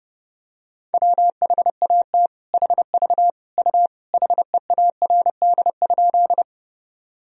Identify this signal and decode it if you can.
Morse code, heard as a keyed tone, about 30 wpm, 700 Hz. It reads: WHAT HV U HEARD?
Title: Sentences from Top 400 Words and Abbreviations 30wpm